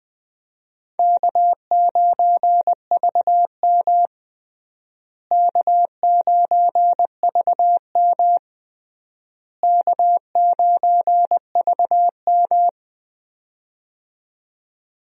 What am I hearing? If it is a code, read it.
Morse code, heard as a keyed tone, about 21 wpm, 700 Hz. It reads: K9VM K9VM K9VM
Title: USA Call Signs - 4 Characters - Repeated 3x - 20wpm